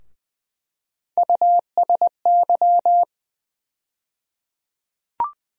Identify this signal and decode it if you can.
Morse code, heard as a keyed tone, about 21 wpm, 700 Hz. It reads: USY
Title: USA Call Sign Postfixes - 3 Letters 20wpm